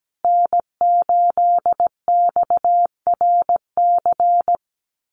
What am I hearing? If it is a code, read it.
Morse code, heard as a keyed tone, about 17 wpm, 700 Hz. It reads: N8XRC
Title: USA Call Signs Repeated 3x 17wpm